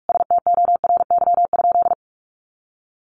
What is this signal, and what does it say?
Morse code, heard as a keyed tone, about 44 wpm, 700 Hz. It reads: STORY?
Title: Sentences from Top 1100 Words 45wpm